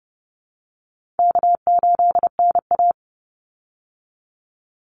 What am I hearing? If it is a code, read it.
Morse code, heard as a keyed tone, about 30 wpm, 700 Hz. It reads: K8NA
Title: Easy Call Signs Repeated 2x - 30wpm